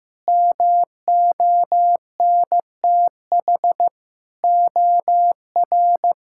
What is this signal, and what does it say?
Morse code, heard as a keyed tone, about 15 wpm, 700 Hz. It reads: MONTH OR
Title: Sentences from Top 1000 Words - Review Part 5 - 15wpm